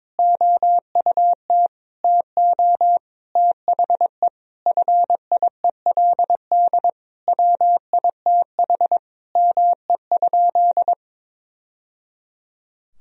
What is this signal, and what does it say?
Morse code, heard as a keyed tone, about 22 wpm, 700 Hz. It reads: OUT TO THE FIELD WITH ME?
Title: Sentences from Top 1000 Words - Review Part 2 - 22wpm